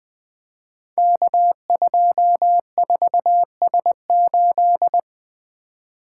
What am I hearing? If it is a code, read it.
Morse code, heard as a keyed tone, about 20 wpm, 700 Hz. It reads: K24S8
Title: Code Groups - 5 Letter:Numbers - 20wpm